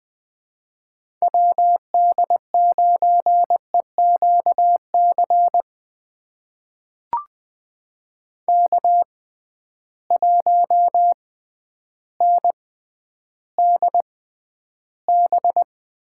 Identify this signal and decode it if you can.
Morse code, heard as a keyed tone, about 20 wpm, spaced as for 11 wpm, 700 Hz. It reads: WD9EQC K1NDB
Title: USA Call Signs - 6x Character Spacing - 20wpm